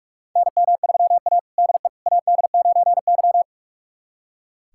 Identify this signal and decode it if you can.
Morse code, heard as a keyed tone, about 44 wpm, 690 Hz. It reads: NM3A DE AD9Y
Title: USA Callsign DE Callsign - 4 Characters - 45wpm